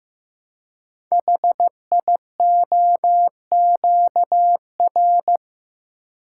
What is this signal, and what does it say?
Morse code, heard as a keyed tone, about 15 wpm, 700 Hz. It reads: HIOQR
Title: Code Groups - 5 Letter:Numbers - 15wpm